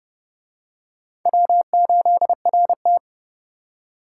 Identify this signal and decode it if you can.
Morse code, heard as a keyed tone, about 30 wpm, 700 Hz. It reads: W8RT